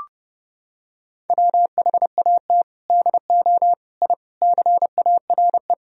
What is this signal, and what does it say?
Morse code, heard as a keyed tone, about 30 wpm, 700 Hz. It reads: WHAT DO I CARE
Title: Sentences from Top 1000 Words - Review Part 1 - 30wpm